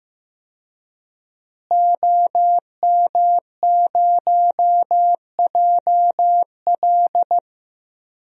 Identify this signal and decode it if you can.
Morse code, heard as a keyed tone, about 15 wpm, 700 Hz. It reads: OM0JL
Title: DXCC-100 Call Signs - 5 Characters - Repeated 3x - 15wpm